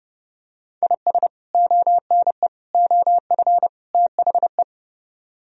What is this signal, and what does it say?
Morse code, heard as a keyed tone, about 32 wpm, 700 Hz. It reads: IS ONE OF THE